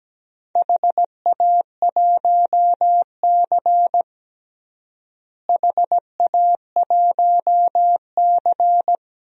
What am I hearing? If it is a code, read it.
Morse code, heard as a keyed tone, about 17 wpm, 700 Hz. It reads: HA1C HA1C
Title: DXCC-100 Call Signs - 4 Characters - Repeated 3x - 17wpm